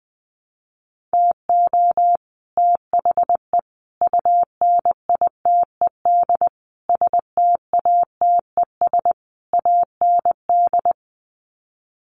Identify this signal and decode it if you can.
Morse code, heard as a keyed tone, about 20 wpm, 700 Hz. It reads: TO THE UNITED STATES AND
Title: Sets of 5 Words Encore 20wpm